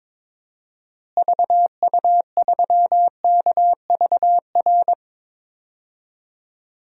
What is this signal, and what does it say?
Morse code, heard as a keyed tone, about 22 wpm, 700 Hz. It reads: VU3KVR
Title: DXCC-100 Call Signs - 6 Characters - Repeated 3x - 22wpm